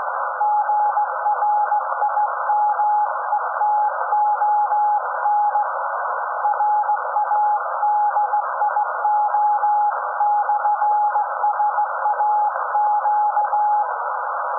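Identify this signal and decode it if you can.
Morse code, heard as a keyed tone, about 16 wpm, 840 Hz, with noise. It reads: GRIMETON RADIO/SAQ
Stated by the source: SAQ Grimeton Radio, Alexanderson 200 kW alternator on 17.2 kHz, received in Potsdam with a very strong signal on World Radio Day.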